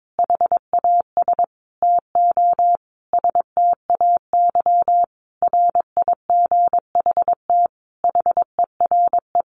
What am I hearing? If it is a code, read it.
Morse code, heard as a keyed tone, about 22 wpm, 700 Hz. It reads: HAS TO STAY RIGHT HERE
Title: A Sound Of Thunder 22wpm